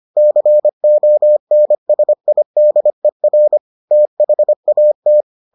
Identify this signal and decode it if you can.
Morse code, heard as a keyed tone, about 25 wpm, 600 Hz. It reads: CONSIDER THAT